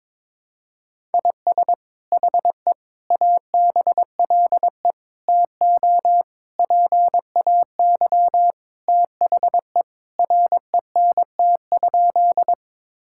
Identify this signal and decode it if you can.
Morse code, heard as a keyed tone, about 21 wpm, 700 Hz. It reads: IS HE ABLE TO PAY THE RENT?
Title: Sentences from Top 1000 Words - Review Part 5 - 22wpm